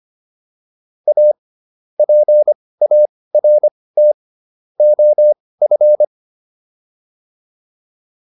Morse code, heard as a keyed tone, about 25 wpm, spaced as for 18 wpm, 600 Hz. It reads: A PART OF